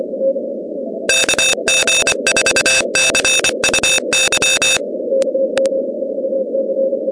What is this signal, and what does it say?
Morse code, heard as a keyed tone, about 25 wpm, 2600 Hz, with noise. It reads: KG4CUY
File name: contests / 2019-cw-wpx-kt5j-k5pi- -2330 0-17